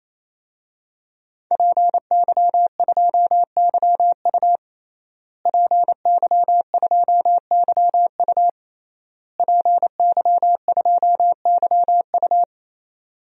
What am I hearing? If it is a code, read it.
Morse code, heard as a keyed tone, about 28 wpm, 700 Hz. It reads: PY2YU PY2YU PY2YU
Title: DXCC-100 Call Signs - 5 Characters - Repeated 3x - 28wpm